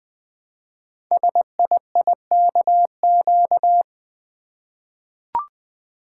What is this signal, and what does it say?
Morse code, heard as a keyed tone, about 20 wpm, 700 Hz. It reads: SIIKQ
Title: Code Groups - 5 Letters - 20wpm